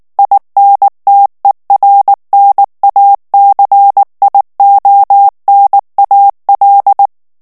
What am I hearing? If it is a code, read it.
Morse code, heard as a keyed tone, about 19 wpm, 800 Hz. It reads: INTERNACIONAL
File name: morse mp3 / ESPAGNOL TEXTE EN CLAIR 2 VITESSE 18 MOTS MINUTES